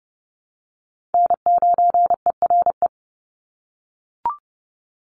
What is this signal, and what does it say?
Morse code, heard as a keyed tone, about 29 wpm, 700 Hz. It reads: N9ERE
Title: USA Call Signs 30wpm